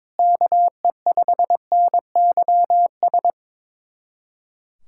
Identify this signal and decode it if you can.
Morse code, heard as a keyed tone, about 22 wpm, 700 Hz. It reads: KE5NYS
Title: USA Callsign DE Callsign 22wpm